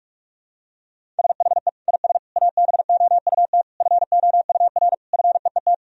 Morse code, heard as a keyed tone, about 46 wpm, 700 Hz. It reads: SHE IS ABOUT FOUR FEET
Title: Sentences from Top 1000 Words - Review Part 1 - 45wpm